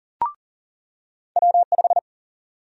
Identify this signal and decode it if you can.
Morse code, heard as a keyed tone, about 40 wpm, 700 Hz. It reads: W5
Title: USA Call Sign Prefixes - 40wpm